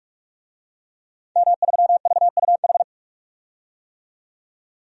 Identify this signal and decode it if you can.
Morse code, heard as a keyed tone, about 43 wpm, 700 Hz. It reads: M3VUH
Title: DXCC-100 Call Signs - 5 Characters - 45wpm